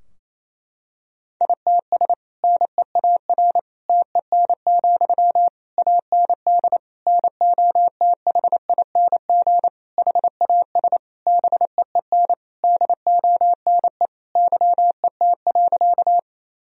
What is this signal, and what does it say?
Morse code, heard as a keyed tone, about 28 wpm, 700 Hz. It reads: ITS NEAR TEN, AND NOTHING HAS BEEN DONE YET.
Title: Sentences from Top 1000 Words - Review Part 2 - 28wpm